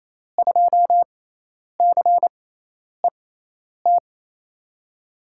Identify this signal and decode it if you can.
Morse code, heard as a keyed tone, about 28 wpm, spaced as for 10 wpm, 700 Hz. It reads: 2CET
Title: USA Call Signs - 6x Character Spacing - 28wpm